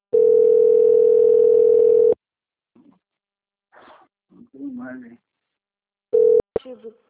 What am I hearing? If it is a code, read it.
Ringback tone (North America), 440 + 480 Hz, with speech.
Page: Jacarei - SP